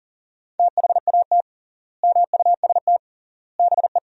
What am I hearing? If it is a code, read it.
Morse code, heard as a keyed tone, about 40 wpm, 700 Hz. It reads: THAT MUST BE
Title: 3x Word-Spacing - Sentences from Top 400 Words - 40wpm